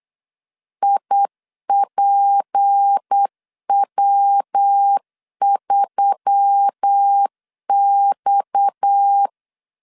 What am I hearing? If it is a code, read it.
Morse code, heard as a keyed tone, about 8 wpm, 800 Hz. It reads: IPW3X
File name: 8wpm